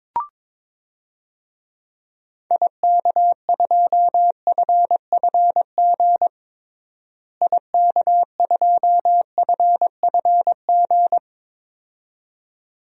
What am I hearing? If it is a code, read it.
Morse code, heard as a keyed tone, about 22 wpm, 700 Hz. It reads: IK2FFG IK2FFG
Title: DXCC-100 Call Signs - 6 Characters - Repeated 2x - 22wpm